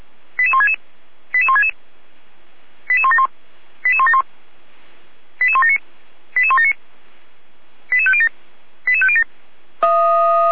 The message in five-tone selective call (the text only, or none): 88166 88166 88171 88171 88179 88179 88507 88507